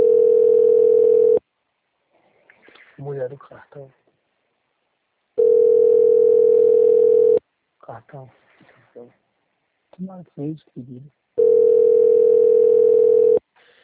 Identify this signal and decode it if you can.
Ringback tone (North America), 440 + 480 Hz, with speech.
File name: out- - - - - .31717